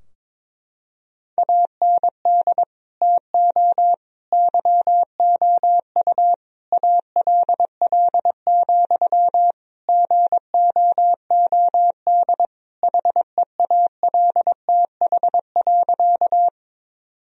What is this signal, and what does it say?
Morse code, heard as a keyed tone, about 22 wpm, 700 Hz. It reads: AND TO YOU ALL, GOOD HEALTH.
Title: Sentences from Top 1000 Words - Review Part 3 - 22wpm